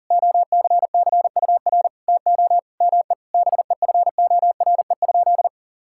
Morse code, heard as a keyed tone, about 40 wpm, 700 Hz. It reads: OCCUR TO ME BEFORE?
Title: Sentences from Top 1000 Words - Review Part 6 - 40wpm